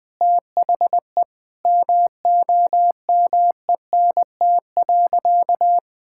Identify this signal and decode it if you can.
Morse code, heard as a keyed tone, about 21 wpm, 700 Hz. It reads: THE MOMENT.
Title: Sentences from Top 1000 Words - Review Part 6 - 20wpm